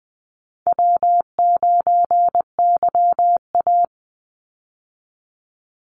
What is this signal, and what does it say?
Morse code, heard as a keyed tone, about 20 wpm, 700 Hz. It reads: W9YA
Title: Easy Call Signs Repeated 3x - 20wpm